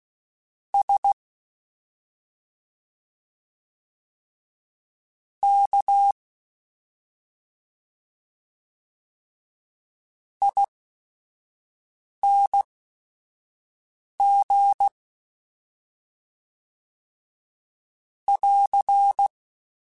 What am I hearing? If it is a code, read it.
Morse code, heard as a keyed tone, about 16 wpm, spaced as for 5 wpm, 800 Hz. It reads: S K ING <AR>
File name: Special INGs / Special INGs-mp3s / SlowToFaster ing @5